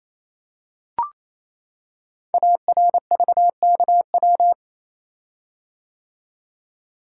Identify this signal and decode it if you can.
Morse code, heard as a keyed tone, about 28 wpm, 700 Hz. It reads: ARVKW
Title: Code Groups - 5 Letters - 28wpm